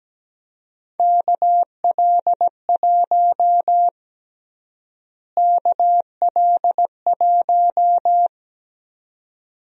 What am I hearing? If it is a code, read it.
Morse code, heard as a keyed tone, about 17 wpm, 700 Hz. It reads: KL1 KL1
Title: USA Call Sign Prefixes - Repeated 2x - 17wpm